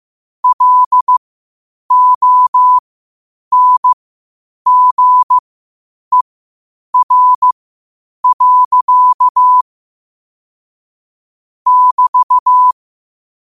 Morse code, heard as a keyed tone, about 15 wpm, spaced as for 8 wpm, 1000 Hz. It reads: LONGER. <BT>
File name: qotdfiles / 8wpm